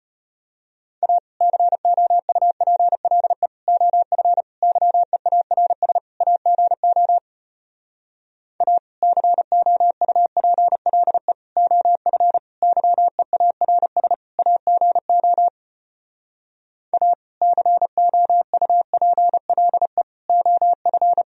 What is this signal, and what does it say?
Morse code, heard as a keyed tone, about 33 wpm, 700 Hz. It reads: A COUPLE OF YEARS AGO A COUPLE OF YEARS AGO A COUPLE OF